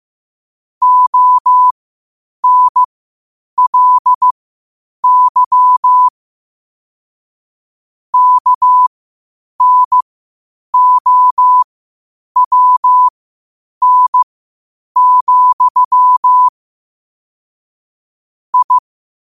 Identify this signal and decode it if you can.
Morse code, heard as a keyed tone, about 15 wpm, spaced as for 8 wpm, 1000 Hz. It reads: ONLY KNOWN, I
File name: qotdfiles / 8wpm